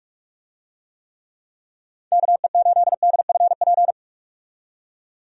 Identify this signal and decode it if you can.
Morse code, heard as a keyed tone, about 47 wpm, 700 Hz. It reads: KE8DFP
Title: USA Call Signs - 6 Characters - 45wpm